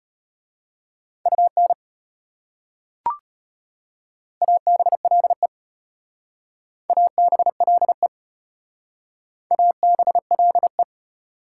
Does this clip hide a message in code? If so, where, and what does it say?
Yes: Morse code from 1.3 to 10.8 s, heard as a keyed tone, about 34 wpm, 700 Hz. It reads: UN ABLE ABLE ABLE